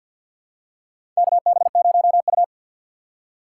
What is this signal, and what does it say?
Morse code, heard as a keyed tone, about 52 wpm, 690 Hz. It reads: KB0U
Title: USA Call Signs - 2x Character Spacing - 50wpm